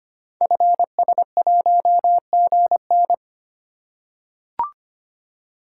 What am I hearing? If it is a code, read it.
Morse code, heard as a keyed tone, about 25 wpm, 700 Hz. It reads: FS1GN
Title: Code Groups - 5 Letter:Numbers - 25wpm